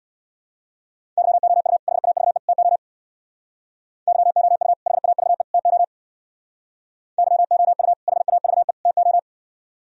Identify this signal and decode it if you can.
Morse code, heard as a keyed tone, about 64 wpm, 700 Hz. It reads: YOU HAVE TO YOU HAVE TO YOU HAVE TO